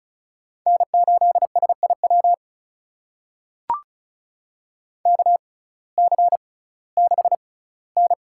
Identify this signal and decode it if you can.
Morse code, heard as a keyed tone, about 34 wpm, spaced as for 19 wpm, 700 Hz. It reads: N8SIW KC6N
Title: USA Call Signs - 6x Character Spacing - 35wpm